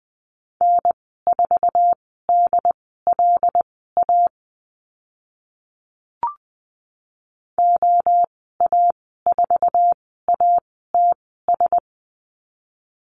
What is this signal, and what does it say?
Morse code, heard as a keyed tone, about 20 wpm, spaced as for 15 wpm, 700 Hz. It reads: N4DLA OA4ATS
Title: 076 Lesson 13g - Letter L - Call Signs 20wpm